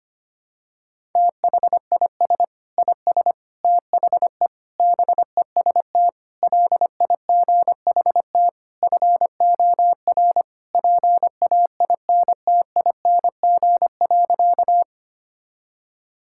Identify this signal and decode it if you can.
Morse code, heard as a keyed tone, about 25 wpm, 700 Hz. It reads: THIS IS THE BEST LIGHT FOR PAINTING.
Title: Sentences from Top 1000 Words - Review Part 6 - 25wpm